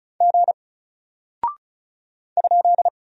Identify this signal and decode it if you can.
Morse code, heard as a keyed tone, about 36 wpm, 700 Hz. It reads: G ?
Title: Single Character (Letter:Number:Punctuation:Prosign) Alt 35wpm